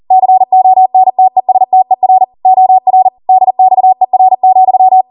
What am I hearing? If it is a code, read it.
Morse code, heard as a keyed tone, about 40 wpm, 750 Hz. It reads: CONTESTER OR DXER,